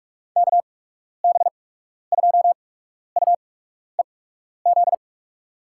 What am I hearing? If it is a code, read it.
Morse code, heard as a keyed tone, about 46 wpm, spaced as for 12 wpm, 700 Hz. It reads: KB2UEZ
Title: USA Call Signs - 8x Character Spacing - 45wpm